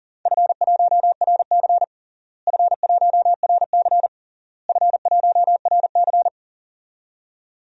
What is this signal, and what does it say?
Morse code, heard as a keyed tone, about 40 wpm, 700 Hz. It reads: F1RC F1RC F1RC
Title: DXCC-100 Call Signs - 4 Characters - Repeated 3x - 40wpm